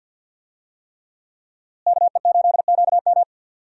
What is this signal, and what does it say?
Morse code, heard as a keyed tone, about 48 wpm, 700 Hz. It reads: KE8QM
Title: USA Call Signs - 5 Characters - Repeated 3x - 50wpm